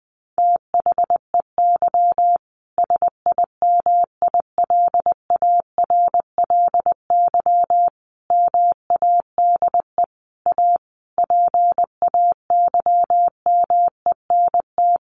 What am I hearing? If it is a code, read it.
Morse code, heard as a keyed tone, about 20 wpm, 700 Hz. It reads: THEY SIMILARLY MADE A PAYMENT
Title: Sentences from Top 1400 Words 20wpm